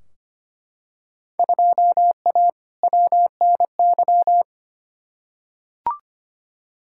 Morse code, heard as a keyed tone, about 25 wpm, 700 Hz. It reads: 2A WNY
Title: Field Day Exchanges - 25wpm